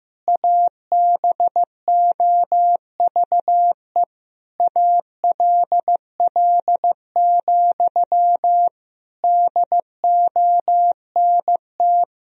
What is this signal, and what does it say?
Morse code, heard as a keyed tone, about 15 wpm, 700 Hz. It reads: ABOVE ALL, DONT
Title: Sentences from Top 1000 Words - Review Part 2 - 15wpm